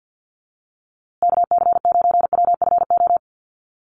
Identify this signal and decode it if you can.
Morse code, heard as a keyed tone, about 50 wpm, 700 Hz. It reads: KC9WFO
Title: USA Call Signs Repeated 3x 50wpm